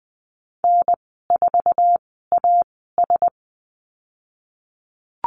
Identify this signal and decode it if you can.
Morse code, heard as a keyed tone, about 20 wpm, spaced as for 15 wpm, 700 Hz. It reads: N4AS
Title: 048 Lesson 9g - Number 4 - Call Signs 20wpm